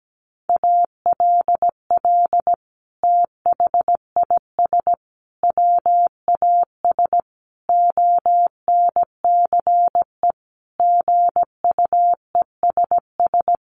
Morse code, heard as a keyed tone, about 17 wpm, 700 Hz. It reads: ALL THIS WAS ONCE GUESS
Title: Sentences from Top 1500 Words 17wpm